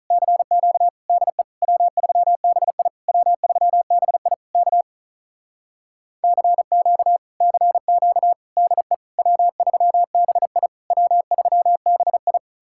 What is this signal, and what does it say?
Morse code, heard as a keyed tone, about 44 wpm, 700 Hz. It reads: CQ DE W3BI W3BI K CQ CQ DE W3BI W3BI
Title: Speed-Racing Common QSO Phrases - 35wpm